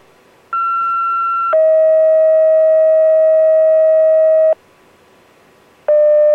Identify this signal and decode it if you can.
Two-tone page, heard as a one-second tone, then a three-second tone, about 1360 then 640 Hz. It is heard from 0.5 to 4.5 s.